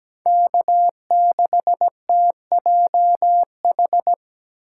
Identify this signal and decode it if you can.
Morse code, heard as a keyed tone, about 17 wpm, 700 Hz. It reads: K6TJH
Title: USA Call Signs - 5 Characters - 2x Character Spacing - 17wpm